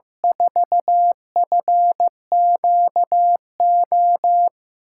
Morse code, heard as a keyed tone, about 15 wpm, 700 Hz. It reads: 4FQO
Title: USA Call Signs - 6 Characters - 8x Character Spacing - 15wpm